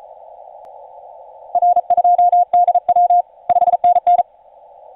Morse code, heard as a keyed tone, about 38 wpm, 710 Hz. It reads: R2DW 5NN
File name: 5u5r 17m cw